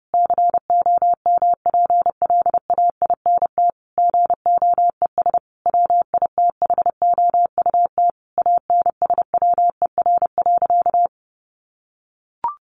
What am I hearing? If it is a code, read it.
Morse code, heard as a keyed tone, about 28 wpm, 700 Hz. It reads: COMPLAINT GOES WITHOUT ANSWER.
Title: Sentences from Top 1500 Words 30wpm